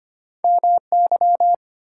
Morse code, heard as a keyed tone, about 33 wpm, 700 Hz. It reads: MY